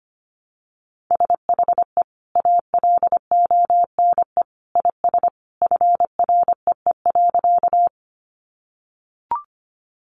Morse code, heard as a keyed tone, about 25 wpm, 700 Hz. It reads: SHE ALONE IS FREE.
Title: Sentences from Top 1000 Words 25wpm